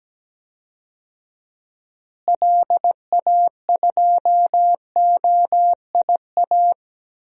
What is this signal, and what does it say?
Morse code, heard as a keyed tone, about 17 wpm, 700 Hz. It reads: LA2OIA
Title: DXCC-100 Call Signs - 6 Characters - Repeated 2x - 17wpm